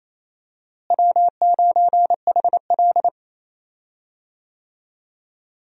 Morse code, heard as a keyed tone, about 28 wpm, 700 Hz. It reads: W9HL
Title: USA Call Signs - 4 Characters - Repeated 2x - 28wpm